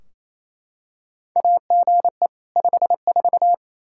Morse code, heard as a keyed tone, about 28 wpm, 700 Hz. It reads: AGE 54
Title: Practice QSO - Exchange 3 - More Info - 28wpm